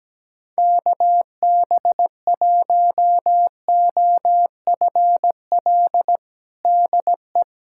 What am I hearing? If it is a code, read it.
Morse code, heard as a keyed tone, about 17 wpm, 700 Hz. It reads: KB1OFL DE
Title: USA Callsign DE Callsign - 6 Characters - 17wpm